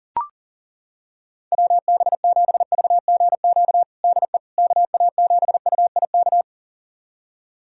Morse code, heard as a keyed tone, about 40 wpm, 700 Hz. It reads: WB7VGQ DE KA7UIK
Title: USA Callsign DE Callsign - 6 Characters - 40wpm